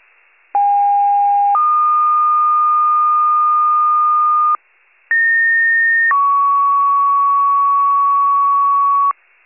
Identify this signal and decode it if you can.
Two-tone page, heard as a one-second tone, then a three-second tone, about 810 then 1220 Hz, about 1850 then 1120 Hz. It is heard from 0.5 to 9.1 s.
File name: wcalert